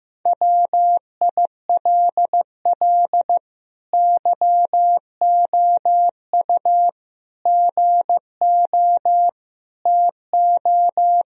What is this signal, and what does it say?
Morse code, heard as a keyed tone, about 15 wpm, 700 Hz. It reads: WILL YOU GO TO
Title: Sentences from Top 1000 Words - Review Part 2 - 15wpm